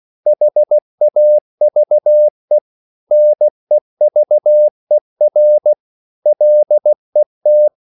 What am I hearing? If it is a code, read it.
Morse code, heard as a keyed tone, about 16 wpm, 600 Hz. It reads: HAVE NEVER LET
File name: CW / Practice / EnglishQuotations0416